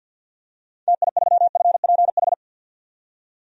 Morse code, heard as a keyed tone, about 47 wpm, 700 Hz. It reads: TI3FPH
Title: DXCC-100 Call Signs - 6 Characters - 50wpm